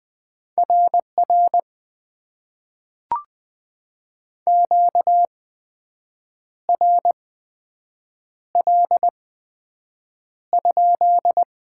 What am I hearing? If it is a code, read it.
Morse code, heard as a keyed tone, about 20 wpm, spaced as for 7 wpm, 700 Hz. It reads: RR QRL?
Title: Top 90 QSO Elements - 8x Character Spacing - 20wpm